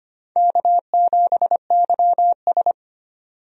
Morse code, heard as a keyed tone, about 25 wpm, 700 Hz. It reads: K7YS
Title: USA Call Signs - 4 Characters - 25wpm